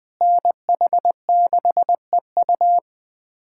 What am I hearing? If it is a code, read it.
Morse code, heard as a keyed tone, about 20 wpm, 700 Hz. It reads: NH6EU
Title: USA Callsign DE Callsign - 5 Characters - 20wpm